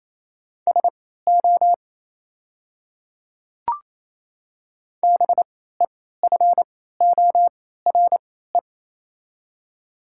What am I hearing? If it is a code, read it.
Morse code, heard as a keyed tone, about 27 wpm, spaced as for 16 wpm, 700 Hz. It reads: SO BEFORE
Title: 3x Character Spacing - Top 200 Words - 28wpm